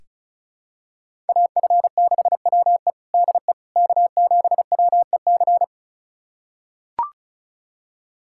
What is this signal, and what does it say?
Morse code, heard as a keyed tone, about 34 wpm, 700 Hz. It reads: AF6WE DE K7WEC
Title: USA Callsign DE Callsign - 5 Characters - 35wpm